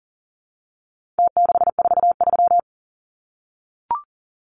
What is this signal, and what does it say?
Morse code, heard as a keyed tone, about 40 wpm, 700 Hz. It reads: T643
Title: Numbers - Contest Style - 4-Digit 40wpm